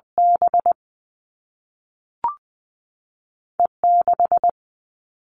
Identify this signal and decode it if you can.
Morse code, heard as a keyed tone, about 20 wpm, 700 Hz. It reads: B E 6
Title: All Cut Numbers - 2-Digit 20wpm